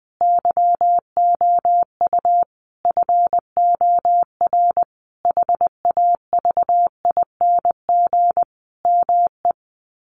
Morse code, heard as a keyed tone, about 20 wpm, 700 Hz. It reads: YOU FOR HAVING ME
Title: Sets of 5 Words Encore 20wpm